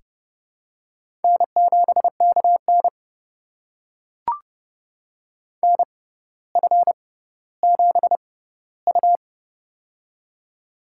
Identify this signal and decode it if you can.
Morse code, heard as a keyed tone, about 30 wpm, spaced as for 15 wpm, 700 Hz. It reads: N7KN NF7U
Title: USA Call Signs - 4 Characters - 6x Character Spacing - 30wpm